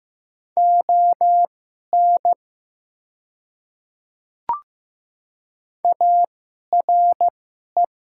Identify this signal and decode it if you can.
Morse code, heard as a keyed tone, about 15 wpm, spaced as for 11 wpm, 700 Hz. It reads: ON ARE
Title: 2x Character Spacing - Top 100 Words - 15wpm